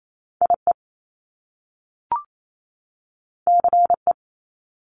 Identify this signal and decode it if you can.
Morse code, heard as a keyed tone, about 28 wpm, 700 Hz. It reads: IE CE
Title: Top Two Letter Combinations in Words 28wpm